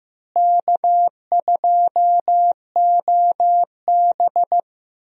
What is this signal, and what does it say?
Morse code, heard as a keyed tone, about 15 wpm, 700 Hz. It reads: K2OB
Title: USA Call Signs - 4 Characters - Repeated 2x - 15wpm